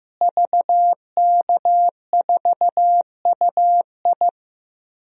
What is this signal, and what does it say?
Morse code, heard as a keyed tone, about 15 wpm, 700 Hz. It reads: VK4UI
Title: DXCC-100 Call Signs - 5 Characters - Repeated 2x - 15wpm